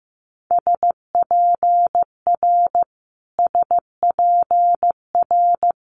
Morse code, heard as a keyed tone, about 16 wpm, 700 Hz. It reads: SPR SPR
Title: Multi-letter Phonemes Repeated 3x 15wpm